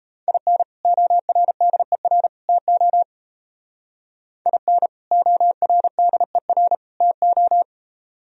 Morse code, heard as a keyed tone, about 32 wpm, 700 Hz. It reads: IN ORDER TO IN ORDER TO
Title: Speed-Racing - Sets of 3 Words 28wpm